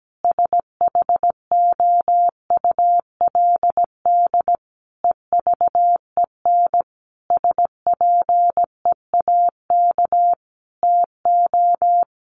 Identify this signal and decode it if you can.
Morse code, heard as a keyed tone, about 17 wpm, 700 Hz. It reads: SHOULD EVEN SPEAK TO
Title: Sentences from Top 1500 Words 17wpm